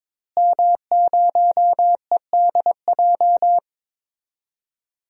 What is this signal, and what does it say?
Morse code, heard as a keyed tone, about 22 wpm, 700 Hz. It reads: M0EDJ
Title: DXCC-100 Call Signs - 5 Characters - Repeated 3x - 22wpm